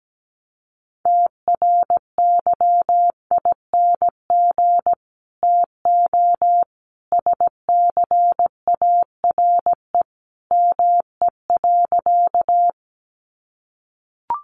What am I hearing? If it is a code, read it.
Morse code, heard as a keyed tone, about 17 wpm, 700 Hz. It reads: TRYING TO SCARE ME.
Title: A Sound Of Thunder 17wpm